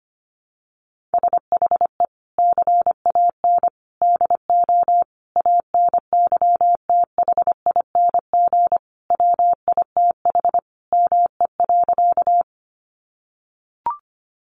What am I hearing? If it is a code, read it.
Morse code, heard as a keyed tone, about 25 wpm, 700 Hz. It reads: SHE CAN DO ANYTHING WITH ME.
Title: Sentences from Top 1000 Words 25wpm